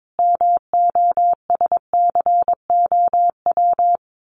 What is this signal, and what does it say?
Morse code, heard as a keyed tone, about 22 wpm, 700 Hz. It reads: MOSCOW
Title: Common QSO Phrases 22wpm